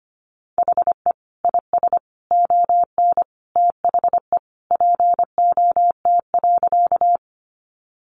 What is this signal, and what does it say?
Morse code, heard as a keyed tone, about 25 wpm, 700 Hz. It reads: HE IS ON THE POT.
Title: Sentences from Top 1000 Words 25wpm